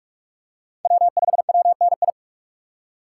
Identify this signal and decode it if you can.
Morse code, heard as a keyed tone, about 44 wpm, 700 Hz. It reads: W5WNI
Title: USA Call Signs - 5 Characters - 6x Character Spacing - 45wpm